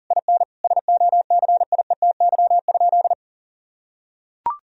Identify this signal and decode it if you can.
Morse code, heard as a keyed tone, about 37 wpm, 700 Hz. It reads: IN SOCIETY?
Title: Sentences from Top 1000 Words - Review Part 4 - 40wpm